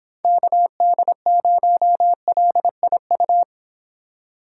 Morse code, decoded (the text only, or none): KD0LIU